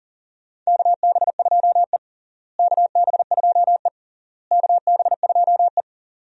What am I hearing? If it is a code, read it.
Morse code, heard as a keyed tone, about 40 wpm, 700 Hz. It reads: KB2E KB2E KB2E
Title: USA Call Signs - 4 Characters - Repeated 3x - 40wpm